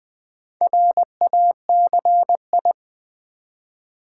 Morse code, heard as a keyed tone, about 20 wpm, 700 Hz. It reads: RACI
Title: Build A Word - 2 Letters At A Time - 6-Letter Words - 20wpm